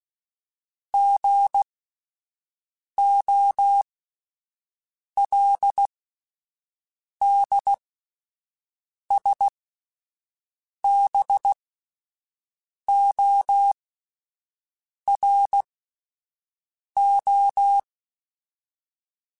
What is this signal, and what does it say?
Morse code, heard as a keyed tone, about 16 wpm, spaced as for 5 wpm, 800 Hz. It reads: GOLDSBORO
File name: fromSMs / FromSMs mp3s / Cities@5